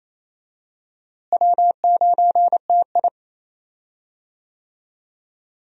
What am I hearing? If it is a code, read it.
Morse code, heard as a keyed tone, about 27 wpm, 700 Hz. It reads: W9TI
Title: USA Call Signs - 4 Characters - Repeated 3x - 28wpm